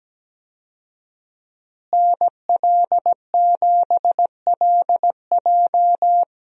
Morse code, heard as a keyed tone, about 17 wpm, 700 Hz. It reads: NL7LJ
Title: USA Call Signs - 5 Characters - Repeated 2x - 17wpm